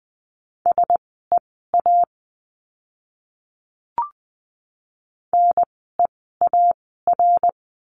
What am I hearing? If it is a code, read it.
Morse code, heard as a keyed tone, about 20 wpm, spaced as for 14 wpm, 700 Hz. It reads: SEA NEAR
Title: 054 Lesson 10f - Letter r - Words 20wpm